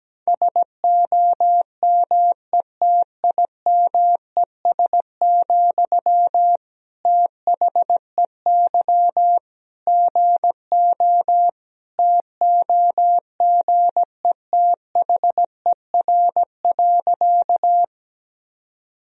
Morse code, heard as a keyed tone, about 17 wpm, 700 Hz. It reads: SOMETIMES, THEY GO TOGETHER.
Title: Sentences from Top 1000 Words - Review Part 3 - 17wpm